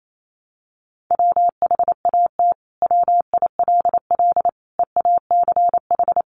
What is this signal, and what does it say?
Morse code, heard as a keyed tone, about 28 wpm, 700 Hz. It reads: WHAT WILL EACH
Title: Sentences from Top 200 Words 28wpm